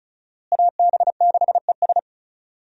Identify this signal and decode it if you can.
Morse code, heard as a keyed tone, about 36 wpm, 700 Hz. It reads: AB6ES
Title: USA Call Signs - 5 Characters - Repeated 2x - 35wpm